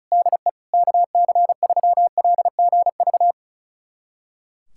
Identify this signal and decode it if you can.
Morse code, heard as a keyed tone, about 34 wpm, 700 Hz. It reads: DE KC3LGV
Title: USA Callsign DE Callsign - 6 Characters - 35wpm